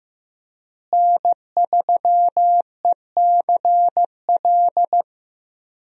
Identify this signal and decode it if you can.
Morse code, heard as a keyed tone, about 15 wpm, 700 Hz. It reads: N3ECL
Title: USA Call Signs - 5 Characters - Repeated 3x - 15wpm